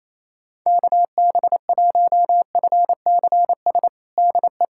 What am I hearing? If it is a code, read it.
Morse code, heard as a keyed tone, about 28 wpm, 700 Hz. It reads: KB1FCS DE
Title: USA Callsign DE Callsign 28wpm